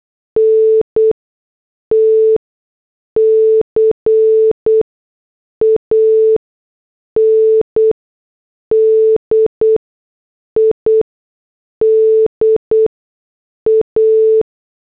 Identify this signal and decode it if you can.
Morse code, heard as a keyed tone, about 8 wpm, 440 Hz. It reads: N T C A N D I D A
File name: morse / headlines